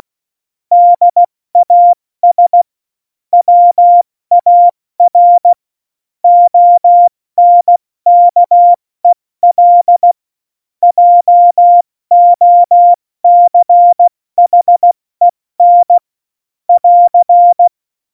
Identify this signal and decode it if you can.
Morse code, heard as a keyed tone, about 16 wpm, 700 Hz. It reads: DAS WAR ONKEL JOCHEN <AR>